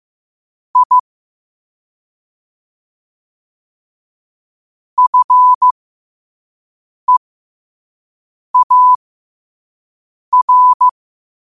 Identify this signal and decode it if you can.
Morse code, heard as a keyed tone, about 15 wpm, spaced as for 5 wpm, 1000 Hz. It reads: I FEAR
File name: qotdfiles / 5wpm